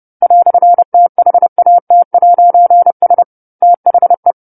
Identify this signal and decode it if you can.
Morse code, heard as a keyed tone, about 30 wpm, 700 Hz. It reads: "THAT'S THE
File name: 10-Chapter-10 30 WPM